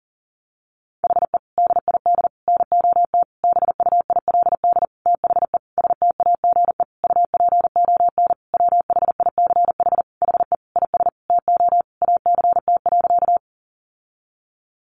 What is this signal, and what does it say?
Morse code, heard as a keyed tone, about 40 wpm, 700 Hz. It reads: HE DID NOT BUILD THE STAGE UPON WHICH HE IS TO ACT.
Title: Sentences from Top 1200 Words 40wpm